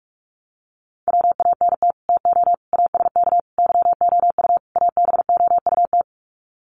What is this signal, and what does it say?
Morse code, heard as a keyed tone, about 44 wpm, 700 Hz. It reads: WANT TO ASK YOU ABOUT
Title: Sets of 5 Words Encore2 45wpm